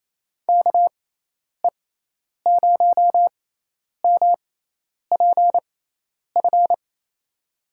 Morse code, heard as a keyed tone, about 27 wpm, spaced as for 10 wpm, 700 Hz. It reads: KE0MPF
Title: USA Call Signs - 6 Characters - 6x Character Spacing - 28wpm